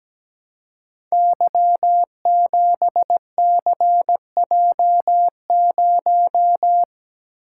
Morse code, heard as a keyed tone, about 17 wpm, 700 Hz. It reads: Y7CJ0
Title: Code Groups - 5 Letter:Numbers - 17wpm